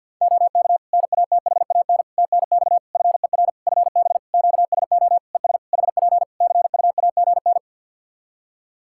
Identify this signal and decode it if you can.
Morse code, heard as a keyed tone, about 51 wpm, 700 Hz. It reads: OK NATHAN TNX FER FB QSO ES HP CUAGN
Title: Practice QSO - Exchange 4 - Ending - 50wpm